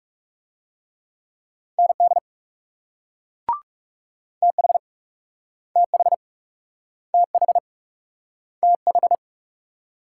Morse code, heard as a keyed tone, about 38 wpm, 700 Hz. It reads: ND TH TH TH TH